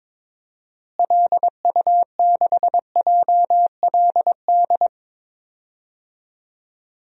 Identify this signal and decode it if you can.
Morse code, heard as a keyed tone, about 22 wpm, 700 Hz. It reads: LU6JLD
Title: DXCC-100 Call Signs - 6 Characters - Repeated 3x - 22wpm